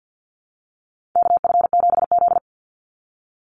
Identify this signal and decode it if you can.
Morse code, heard as a keyed tone, about 50 wpm, 700 Hz. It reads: KF7Z
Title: Call Signs Repeated 3x - 50wpm